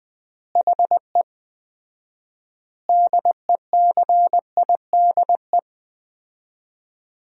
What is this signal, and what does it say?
Morse code, heard as a keyed tone, about 20 wpm, 700 Hz. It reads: HE DECIDE
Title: 4x Word-Spacing - Sentences from Top 500 Words - 20wpm